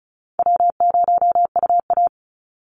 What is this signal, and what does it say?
Morse code, heard as a keyed tone, about 36 wpm, 700 Hz. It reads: W0UA
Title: Call Signs - 35wpm